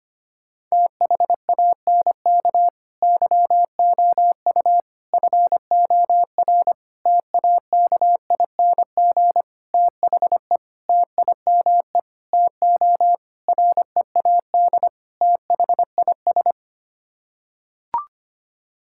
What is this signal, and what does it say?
Morse code, heard as a keyed tone, about 25 wpm, 700 Hz. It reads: THANK YOU FOR TAKING THE TIME TO READ THIS
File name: Sets of 9 Words - 25wpm